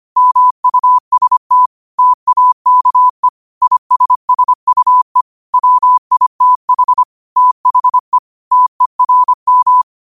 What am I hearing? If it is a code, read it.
Morse code, heard as a keyed tone, about 25 wpm, 1000 Hz. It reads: MUST TAKE ISSUE WITH THE TERM
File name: qotdfiles / 25wpm